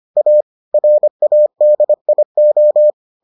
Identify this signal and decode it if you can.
Morse code, heard as a keyed tone, about 25 wpm, 600 Hz. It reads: A RADIO